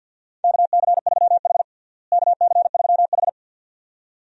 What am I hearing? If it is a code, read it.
Morse code, heard as a keyed tone, about 49 wpm, 700 Hz. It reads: KC3H KC3H
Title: USA Call Signs - 4 Characters - Repeated 2x - 50wpm